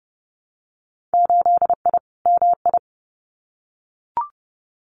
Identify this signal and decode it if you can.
Morse code, heard as a keyed tone, about 30 wpm, 700 Hz. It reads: 8I MI
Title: Winter Field Day Exchanges Encore 30wpm